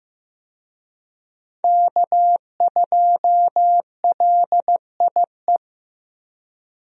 Morse code, heard as a keyed tone, about 15 wpm, 700 Hz. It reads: K2LIE
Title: USA Call Signs - 5 Characters - Repeated 2x - 15wpm